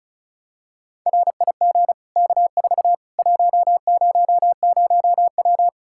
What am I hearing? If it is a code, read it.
Morse code, heard as a keyed tone, about 36 wpm, 700 Hz. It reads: RIG K4 100W
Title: Practice QSO - Exchange 2 - Info - 35wpm